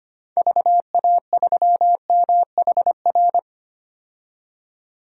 Morse code, heard as a keyed tone, about 25 wpm, 700 Hz. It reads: VA3MHR
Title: DXCC-100 Call Signs - 6 Characters - 25wpm